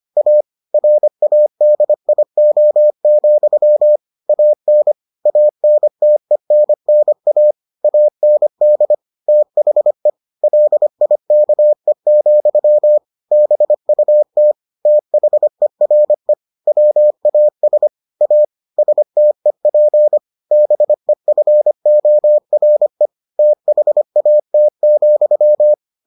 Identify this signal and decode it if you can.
Morse code, heard as a keyed tone, about 25 wpm, 600 Hz. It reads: A RADIO, AN ANTENNA AND THE LIKE, BUT THERE WAS A STEP BEFORE THAT,